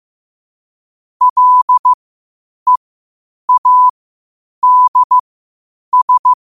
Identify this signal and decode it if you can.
Morse code, heard as a keyed tone, about 15 wpm, spaced as for 8 wpm, 1000 Hz. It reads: LEADS
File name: qotdfiles / 8wpm